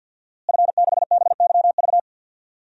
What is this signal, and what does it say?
Morse code, heard as a keyed tone, about 52 wpm, 700 Hz. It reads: F6BYV
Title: DXCC-100 Call Signs - 5 Characters - Repeated 3x - 50wpm